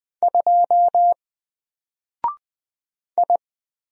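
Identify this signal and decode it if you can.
Morse code, heard as a keyed tone, about 20 wpm, 700 Hz. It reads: 2 I
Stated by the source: Single letter and number Morse drill at 20 wpm, from N4LCD.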